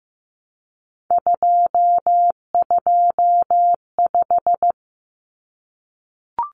Morse code, heard as a keyed tone, about 15 wpm, 700 Hz. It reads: 225
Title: Common Cut Numbers - 3-Digit 15wpm